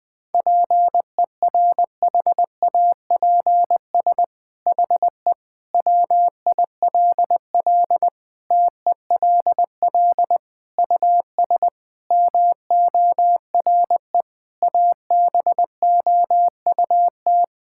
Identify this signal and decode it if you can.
Morse code, heard as a keyed tone, about 21 wpm, 700 Hz. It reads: PERHAPS HE WILL TELL US MORE ABOUT
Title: Sentences from Top 1600 Words 20wpm